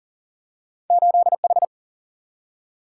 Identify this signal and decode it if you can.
Morse code, heard as a keyed tone, about 39 wpm, 700 Hz. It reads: 8H
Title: Single-Letter Single-Number 40wpm